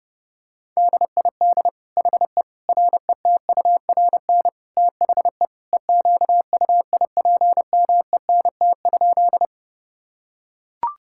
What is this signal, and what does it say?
Morse code, heard as a keyed tone, about 30 wpm, 700 Hz. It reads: DID HE RETURN THE EQUIPMENT?
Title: Sentences from Top 1000 Words - Review Part 5 - 30wpm